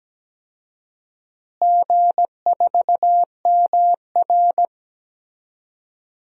Morse code, heard as a keyed tone, about 17 wpm, 700 Hz. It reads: G4MR
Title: DXCC-100 Call Signs - 4 Characters - Repeated 3x - 17wpm